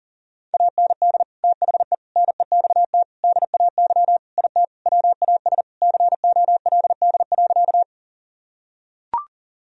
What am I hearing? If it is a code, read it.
Morse code, heard as a keyed tone, about 40 wpm, 700 Hz. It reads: AND THE NEXT DAY IT WAS COLD.
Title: Sentences from Top 1000 Words - Review Part 1 - 40wpm